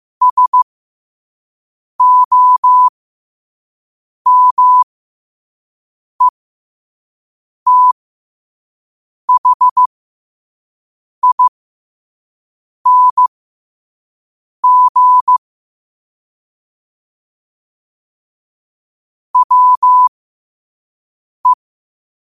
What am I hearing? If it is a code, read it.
Morse code, heard as a keyed tone, about 15 wpm, spaced as for 5 wpm, 1000 Hz. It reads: SOMETHING WE
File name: qotdfiles / 5wpm